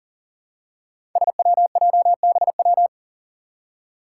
Morse code, heard as a keyed tone, about 40 wpm, 700 Hz. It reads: SWJBW
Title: Five Challenging Letter-Number Combinations 40wpm